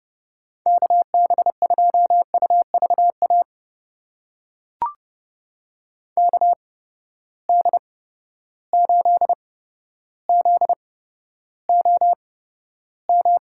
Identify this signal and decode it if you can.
Morse code, heard as a keyed tone, about 30 wpm, spaced as for 13 wpm, 700 Hz. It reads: KB2UVA KD8ZOM
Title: USA Call Signs - 8x Character Spacing - 30wpm